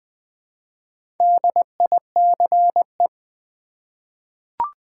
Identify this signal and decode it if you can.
Morse code, heard as a keyed tone, about 20 wpm, 700 Hz. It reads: DICE
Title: Easy Words with Hints - 20wpm